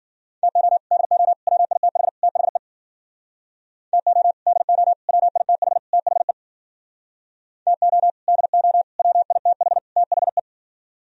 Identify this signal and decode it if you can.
Morse code, heard as a keyed tone, about 52 wpm, 700 Hz. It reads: TO DO WITH THE TO DO WITH THE TO DO WITH THE